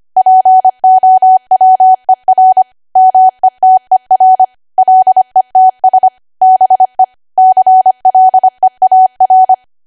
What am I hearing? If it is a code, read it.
Morse code, heard as a keyed tone, about 25 wpm, 750 Hz. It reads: POWER METER LETS BE CLEAR